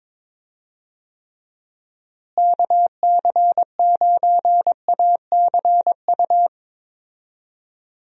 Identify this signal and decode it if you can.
Morse code, heard as a keyed tone, about 22 wpm, 700 Hz. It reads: KC9ACU
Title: USA Call Signs - 6 Characters - Repeated 3x - 22wpm